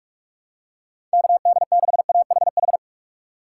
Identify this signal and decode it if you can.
Morse code, heard as a keyed tone, about 46 wpm, 700 Hz. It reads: KD6AHH
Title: USA Call Signs - 3x Character Spacing - 45wpm